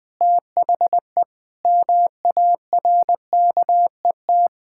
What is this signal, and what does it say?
Morse code, heard as a keyed tone, about 20 wpm, 700 Hz. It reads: THE MARKET
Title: Sentences with 8-Word Phrases - Spoken Beginning - 20wpm